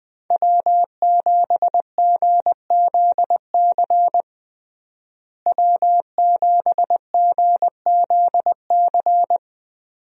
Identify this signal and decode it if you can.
Morse code, heard as a keyed tone, about 21 wpm, 700 Hz. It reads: W7GZC W7GZC
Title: USA Call Signs - 5 Characters - Repeated 2x - 20wpm